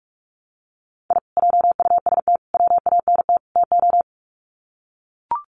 Morse code, heard as a keyed tone, about 44 wpm, 700 Hz. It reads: I JUST WANT TO
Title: Sets of 4 Words 45wpm